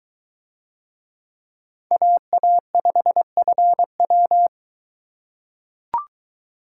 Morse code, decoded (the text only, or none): AA5FW